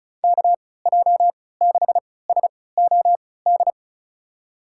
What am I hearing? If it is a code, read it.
Morse code, heard as a keyed tone, about 36 wpm, spaced as for 20 wpm, 700 Hz. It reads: KJ6SOD
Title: USA Call Signs - 6 Characters - 3x Character Spacing - 35wpm